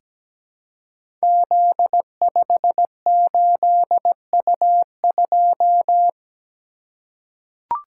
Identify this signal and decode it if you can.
Morse code, heard as a keyed tone, about 17 wpm, 700 Hz. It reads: Z58U2
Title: Code Groups - 5 Letter:Numbers - 17wpm